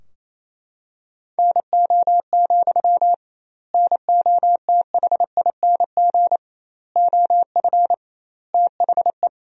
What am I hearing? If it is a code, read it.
Morse code, heard as a keyed tone, about 28 wpm, 700 Hz. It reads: NO, NOTHING OF THE
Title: 2x Word-Spacing - Sentences from Top 500 Words - 28wpm